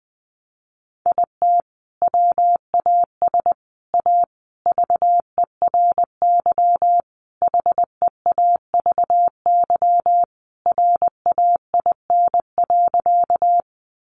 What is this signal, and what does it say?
Morse code, heard as a keyed tone, about 20 wpm, 700 Hz. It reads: IT WAS A VERY HEAVY RAIN.
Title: Sentences from Top 1100 Words 20wpm